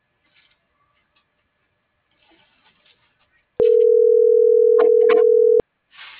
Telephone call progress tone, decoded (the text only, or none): ringback tone